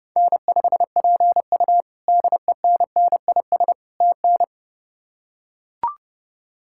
Morse code, heard as a keyed tone, about 30 wpm, 700 Hz. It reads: N5PU DENNIS TN
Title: K1USN SST QSO Practice - 30wpm